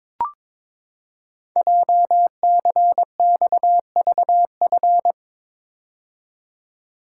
Morse code, heard as a keyed tone, about 23 wpm, 700 Hz. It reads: JCXVF
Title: Code Groups - 5 Letters - 22wpm